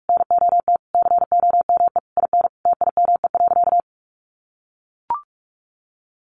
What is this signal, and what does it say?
Morse code, heard as a keyed tone, about 44 wpm, 700 Hz. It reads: NOT COME IN TIME.
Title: Sentences from Top 600 Words 45wpm